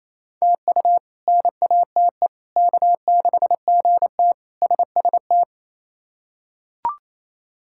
Morse code, heard as a keyed tone, about 27 wpm, 700 Hz. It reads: TU NATE K6GT SST
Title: K1USN SST QSO Practice - 28wpm